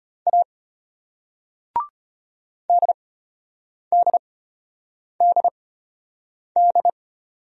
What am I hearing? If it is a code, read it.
Morse code, heard as a keyed tone, about 34 wpm, spaced as for 8 wpm, 700 Hz. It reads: A DDDD